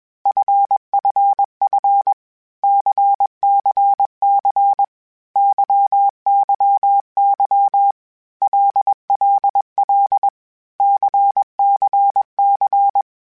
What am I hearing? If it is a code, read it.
Morse code, heard as a keyed tone, about 21 wpm, 800 Hz. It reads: FFF CCC YYY LLL CCC
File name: Basic Sets / Basic Letter Groups Collective7 / LFCYQ@20